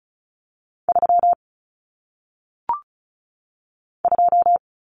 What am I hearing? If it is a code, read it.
Morse code, heard as a keyed tone, about 36 wpm, 700 Hz. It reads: 3 2
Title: Numbers - 1-Digit 35wpm